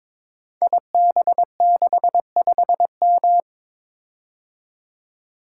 Morse code, heard as a keyed tone, about 22 wpm, 690 Hz. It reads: IB65M
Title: Code Groups - 5 Letter:Numbers - 22wpm